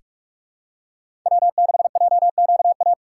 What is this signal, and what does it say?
Morse code, heard as a keyed tone, about 45 wpm, 700 Hz. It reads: W6JQA
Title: USA Call Signs - 5 Characters - 3x Character Spacing - 45wpm